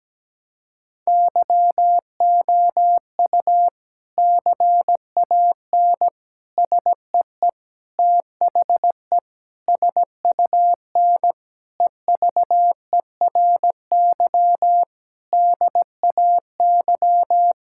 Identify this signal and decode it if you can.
Morse code, heard as a keyed tone, about 17 wpm, 700 Hz. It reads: YOU CAN SEE THE SUN EVERY DAY
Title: Sentences from Top 1600 Words 17wpm